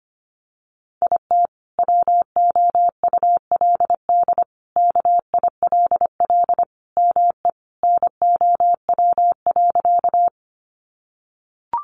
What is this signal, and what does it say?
Morse code, heard as a keyed tone, about 25 wpm, 700 Hz. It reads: IT WOULD KILL ME NOW.
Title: Sentences from Top 1400 Words 25wpm